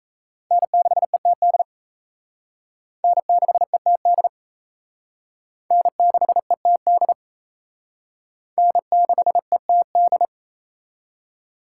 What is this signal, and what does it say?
Morse code, heard as a keyed tone, about 34 wpm, 700 Hz. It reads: N6ETD N6ETD N6ETD N6ETD
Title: Speed Racing - USA 1x3 Call Signs 28wpm